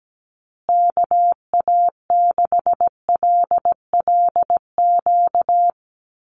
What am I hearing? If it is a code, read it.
Morse code, heard as a keyed tone, about 17 wpm, 700 Hz. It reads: KA6LLQ
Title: USA Call Signs Repeated 3x 17wpm